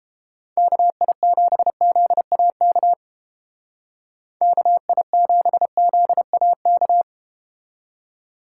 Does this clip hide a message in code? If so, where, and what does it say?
Yes: Morse code from 0.6 to 7.0 s, heard as a keyed tone, about 31 wpm, 700 Hz. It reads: KI7ZAK KI7ZAK